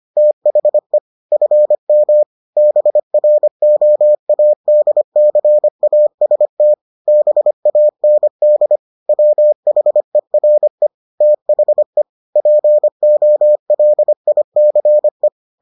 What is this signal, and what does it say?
Morse code, heard as a keyed tone, about 25 wpm, 600 Hz. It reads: THE FM BROADCAST BAND WHERE THE POLICE